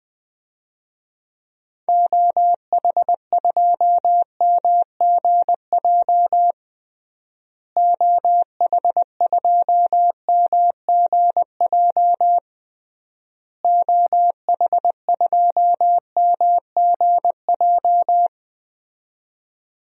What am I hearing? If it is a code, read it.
Morse code, heard as a keyed tone, about 20 wpm, 700 Hz. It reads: OH2MGJ OH2MGJ OH2MGJ
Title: DXCC-100 Call Signs - 6 Characters - Repeated 3x - 20wpm